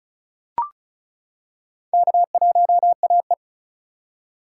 Morse code, heard as a keyed tone, about 36 wpm, 700 Hz. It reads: K1AE
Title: K1USN SST QSO Practice - 35wpm